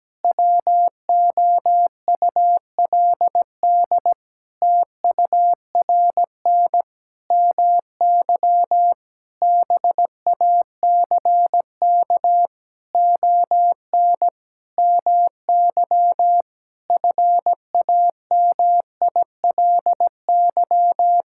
Morse code, heard as a keyed tone, about 17 wpm, 700 Hz. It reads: WOULD TURN MY BACK ON MY FAMILY
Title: Sentences from Top 1000 Words - Review Part 6 - 17wpm